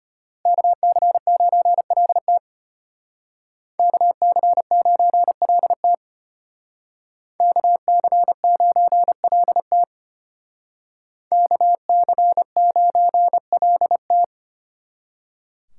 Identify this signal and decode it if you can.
Morse code, heard as a keyed tone, about 31 wpm, 700 Hz. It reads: KC9LT KC9LT KC9LT KC9LT
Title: Speed Racing - USA 2x2 Call Signs 25wpm